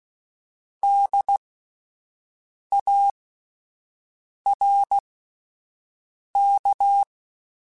Morse code, heard as a keyed tone, about 16 wpm, spaced as for 5 wpm, 800 Hz. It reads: DARK
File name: Misc / Misc-mp3s / Word Pairs @5